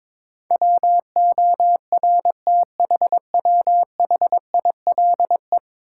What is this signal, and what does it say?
Morse code, heard as a keyed tone, about 22 wpm, 700 Hz. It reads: WORTHWHILE
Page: morse-code-ninja-practice-22wpm